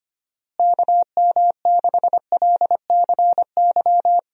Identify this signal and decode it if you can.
Morse code, heard as a keyed tone, about 25 wpm, 700 Hz. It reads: KM6LCY
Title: USA Call Signs - 6 Characters - Repeated 2x - 25wpm